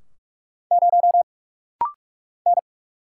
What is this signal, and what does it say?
Morse code, heard as a keyed tone, about 46 wpm, 700 Hz. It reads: 0 N
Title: Single Letter-Number Alt - Rapid-Fire 45wpm